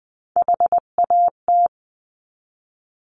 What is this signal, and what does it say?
Morse code, heard as a keyed tone, about 19 wpm, 700 Hz. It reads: HAT
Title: 259 Lesson 42d - Words 18-20wpm